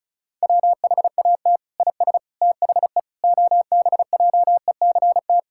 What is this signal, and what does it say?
Morse code, heard as a keyed tone, about 34 wpm, 700 Hz. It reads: WHAT IS THE OBJECT
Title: Sentences from Top 1000 Words - Review Part 6 - 35wpm